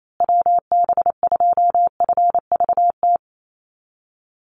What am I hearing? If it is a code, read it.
Morse code, heard as a keyed tone, about 28 wpm, 700 Hz. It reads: WB2FVT
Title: USA Call Signs Repeated 3x 28wpm